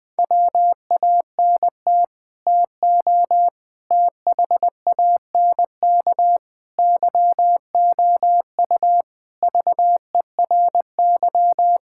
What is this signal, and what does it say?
Morse code, heard as a keyed tone, about 20 wpm, 700 Hz. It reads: WANT TO THANK YOU VERY
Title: Sets of 9 Words - 20wpm